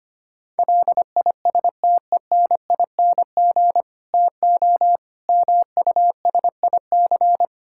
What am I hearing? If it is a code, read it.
Morse code, heard as a keyed tone, about 25 wpm, 700 Hz. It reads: LISTENING TO MUSIC
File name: practice / 25wpm / Difficult Sentences with Hints - 25wpm